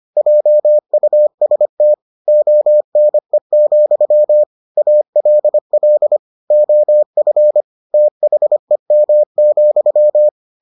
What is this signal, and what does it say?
Morse code, heard as a keyed tone, about 25 wpm, 600 Hz. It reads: JUST ONE, ALL OF THEM,